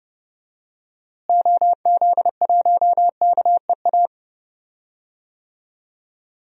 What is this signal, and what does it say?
Morse code, heard as a keyed tone, about 30 wpm, 700 Hz. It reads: OZ1KEA
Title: DXCC-100 Call Signs - 6 Characters - Repeated 2x - 30wpm